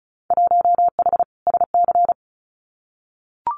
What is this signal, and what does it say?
Morse code, heard as a keyed tone, about 34 wpm, 700 Hz. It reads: 1H SC
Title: Winter Field Day Exchanges 35wpm